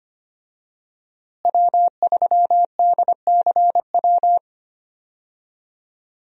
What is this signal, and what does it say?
Morse code, heard as a keyed tone, about 25 wpm, 700 Hz. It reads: W3DCW
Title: USA Call Signs - 5 Characters - Repeated 3x - 25wpm